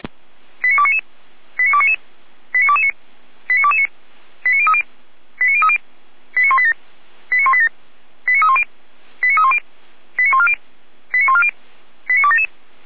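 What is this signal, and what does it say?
Five-tone selective call (ZVEI), heard as quick runs of five tones, tones of about 70 ms. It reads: 89299 89299 89229 89229 89030 89030 89187 89187 89310 89310 89140 89140 89277
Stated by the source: Five-tone ZVEI selective call received by SDR, from the feuerwehr-orlamuende.org fire brigade audio archive.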